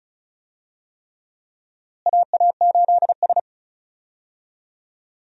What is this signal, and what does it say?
Morse code, heard as a keyed tone, about 34 wpm, 700 Hz. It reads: AA8S
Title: USA Call Signs - 4 Characters - Repeated 3x - 35wpm